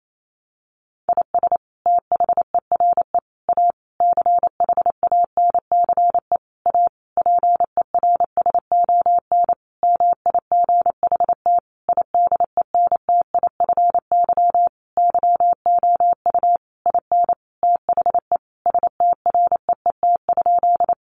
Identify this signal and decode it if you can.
Morse code, heard as a keyed tone, about 28 wpm, 700 Hz. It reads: IS THERE A CHANCE A PERSON MIGHT IDENTIFY YOU IN THE STREET?
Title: Sentences from Top 900 Words 28wpm